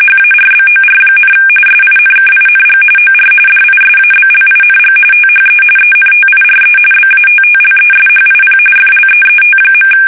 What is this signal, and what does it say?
RTTY, about 45.45 baud. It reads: HARBOR, SINKING OR DISABLING THE ENTIRE BATTLESHIP FLEET.
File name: fsk850